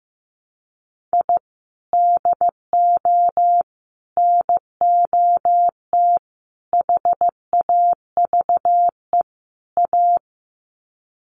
Morse code, heard as a keyed tone, about 15 wpm, 700 Hz. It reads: I DO NOT HAVE A
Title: Sets of 5 Words 15wpm